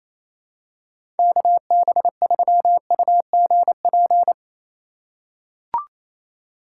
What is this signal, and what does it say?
Morse code, heard as a keyed tone, about 28 wpm, 700 Hz. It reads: KB3UGP
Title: USA Call Signs - 2x Character Spacing - 28wpm